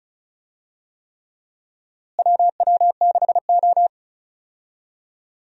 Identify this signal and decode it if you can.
Morse code, heard as a keyed tone, about 34 wpm, 700 Hz. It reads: WW6O
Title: USA Call Signs - 4 Characters - Repeated 3x - 35wpm